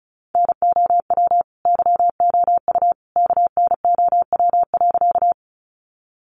Morse code, heard as a keyed tone, about 34 wpm, 700 Hz. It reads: NOW YOU KNOW.
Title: Sentences from Top 100 Words 35wpm